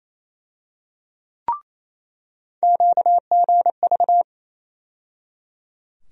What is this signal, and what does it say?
Morse code, heard as a keyed tone, about 27 wpm, 700 Hz. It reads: QGV
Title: USA Call Sign Postfixes - 3 Letters 28wpm